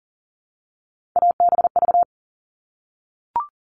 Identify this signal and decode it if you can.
Morse code, heard as a keyed tone, about 40 wpm, 700 Hz. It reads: ABV
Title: All Cut Numbers - 3-Digit 40wpm